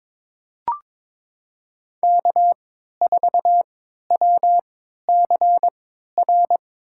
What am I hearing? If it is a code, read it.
Morse code, heard as a keyed tone, about 23 wpm, spaced as for 13 wpm, 700 Hz. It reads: K4WCR
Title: USA Call Signs - 3x Character Spacing - 22wpm